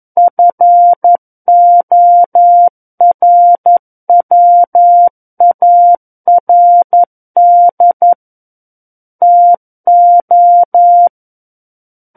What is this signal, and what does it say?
Morse code, heard as a keyed tone, about 11 wpm, 700 Hz. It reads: FORWARD TO
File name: 11 WPM